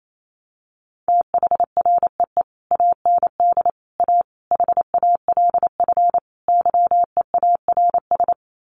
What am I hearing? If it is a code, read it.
Morse code, heard as a keyed tone, about 28 wpm, 700 Hz. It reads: THREE AND A HALF YEARS
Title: Sets of 5 Words Encore2 28wpm